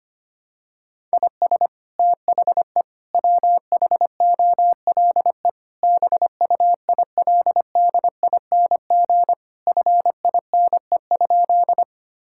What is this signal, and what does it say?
Morse code, heard as a keyed tone, about 25 wpm, 700 Hz. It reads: IS THE WHOLE BUILDING FINE?
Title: Sentences from Top 1000 Words - Review Part 5 - 25wpm